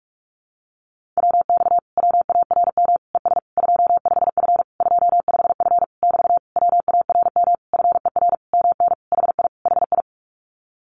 Morse code, heard as a keyed tone, about 44 wpm, 700 Hz. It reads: WX WARM ES 25F 25F <BT> WARM FER MN HI HI
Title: Common QSO Phrases Encore 45wpm